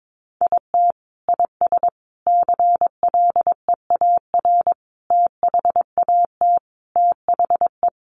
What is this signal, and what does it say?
Morse code, heard as a keyed tone, about 22 wpm, 700 Hz. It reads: IT IS CLEAR THAT THE
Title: Sets of 5 Words Encore 22wpm